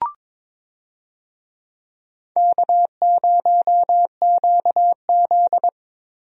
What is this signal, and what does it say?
Morse code, heard as a keyed tone, about 22 wpm, 700 Hz. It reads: K0QZ
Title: USA Call Signs - 4 Characters - Repeated 3x - 22wpm